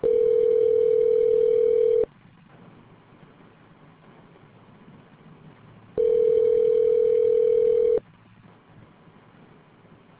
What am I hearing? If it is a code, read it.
Ringback tone (North America), 440 + 480 Hz.